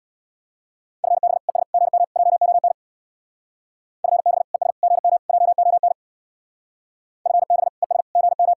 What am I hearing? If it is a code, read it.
Morse code, heard as a keyed tone, about 66 wpm, 700 Hz. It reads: FB ES GM JON FB ES GM JON FB ES GM